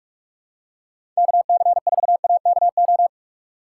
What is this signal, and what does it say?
Morse code, heard as a keyed tone, about 42 wpm, 700 Hz. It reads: KC4AKO
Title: USA Call Signs - 6 Characters - 3x Character Spacing - 45wpm